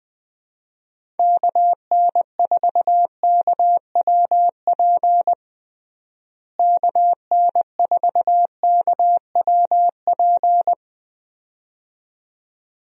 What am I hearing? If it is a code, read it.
Morse code, heard as a keyed tone, about 20 wpm, 700 Hz. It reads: KN4KWP KN4KWP
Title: USA Call Signs - 6 Characters - Repeated 2x - 20wpm